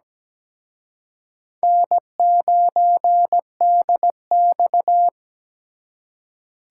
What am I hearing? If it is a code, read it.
Morse code, heard as a keyed tone, about 20 wpm, 700 Hz. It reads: N9DX
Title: Speed Racing - USA 1x2 Call Signs 17wpm